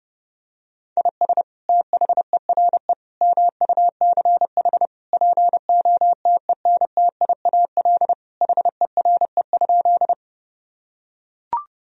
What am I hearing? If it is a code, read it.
Morse code, heard as a keyed tone, about 32 wpm, 700 Hz. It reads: IS THERE MUCH POTENTIAL HERE?
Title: Sentences from Top 1000 Words - Review Part 4 - 30wpm